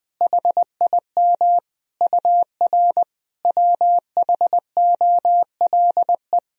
Morse code, heard as a keyed tone, about 20 wpm, 700 Hz. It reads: HIM UR WHOLE
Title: Sentences from Top 400 Words and Abbreviations 20wpm